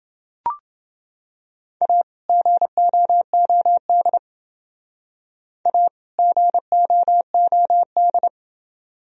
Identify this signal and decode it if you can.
Morse code, heard as a keyed tone, about 29 wpm, 700 Hz. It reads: A GOOD A GOOD